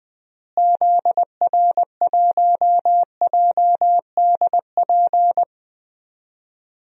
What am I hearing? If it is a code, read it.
Morse code, heard as a keyed tone, about 21 wpm, 700 Hz. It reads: ZR1JDP
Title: DXCC-100 Call Signs - 6 Characters - Repeated 2x - 20wpm